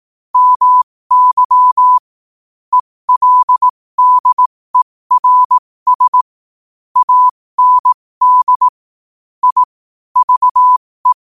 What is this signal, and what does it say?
Morse code, heard as a keyed tone, about 18 wpm, spaced as for 15 wpm, 1000 Hz. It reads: MY ELDERS AND I VE
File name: qotdfiles / 15wpm